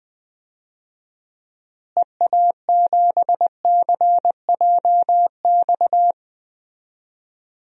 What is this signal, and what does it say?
Morse code, heard as a keyed tone, about 20 wpm, 700 Hz. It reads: EA7CJX
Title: DXCC-100 Call Signs - 6 Characters - Repeated 3x - 20wpm